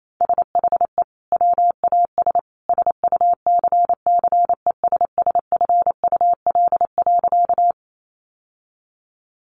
Morse code, heard as a keyed tone, about 28 wpm, 700 Hz. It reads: SHE WAS SUCCESSFUL.
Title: Sentences from Top 900 Words 28wpm